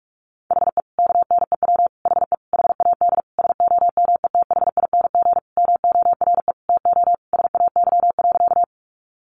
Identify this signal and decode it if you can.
Morse code, heard as a keyed tone, about 44 wpm, 700 Hz. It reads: HE KNEW HE HAD SOMETHING MORE TO SAY.
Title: Sentences from Top 600 Words 45wpm